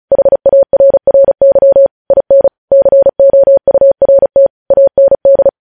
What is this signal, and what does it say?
Morse code, heard as a keyed tone, about 36 wpm, 550 Hz. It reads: HARRY IN COURT AND
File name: podcast / News headlines-1x-35x35